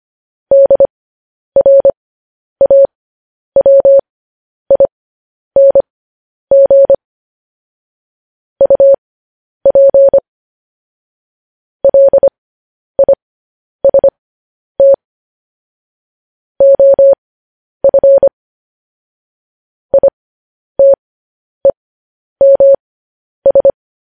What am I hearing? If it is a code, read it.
Morse code, heard as a keyed tone, about 25 wpm, spaced as for 10 wpm, 550 Hz. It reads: DRAWING UP LIST OF ITEMS
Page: news headlines in morse code at 10 wpm